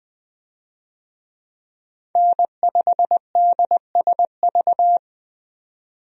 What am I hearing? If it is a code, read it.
Morse code, heard as a keyed tone, about 20 wpm, 700 Hz. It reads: N5DSV
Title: USA Call Signs - 5 Characters - 20wpm